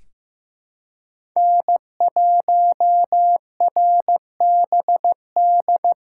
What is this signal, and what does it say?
Morse code, heard as a keyed tone, about 15 wpm, 700 Hz. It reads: N1RBD
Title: K1USN SST QSO Practice - 15wpm